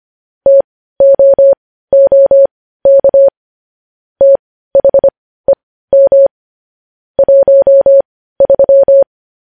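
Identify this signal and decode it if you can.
Morse code, heard as a keyed tone, about 25 wpm, spaced as for 15 wpm, 550 Hz. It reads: TOOK THEM 13
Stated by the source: Morse news headlines sent at 15 words per minute.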